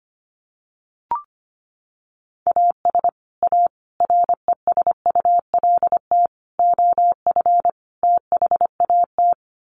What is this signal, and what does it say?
Morse code, heard as a keyed tone, about 25 wpm, 700 Hz. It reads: AS A RESULT OF THAT
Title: Sets of 5 Words Encore 25wpm